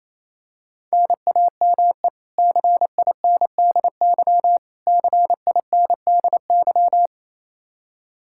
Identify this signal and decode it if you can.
Morse code, heard as a keyed tone, about 28 wpm, 700 Hz. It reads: NAME CINDY CINDY
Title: K1USN SST QSO Practice 28wpm